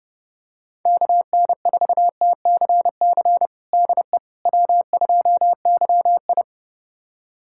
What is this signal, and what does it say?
Morse code, heard as a keyed tone, about 32 wpm, 700 Hz. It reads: KN4TCC DE W2YI
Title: USA Callsign DE Callsign 30wpm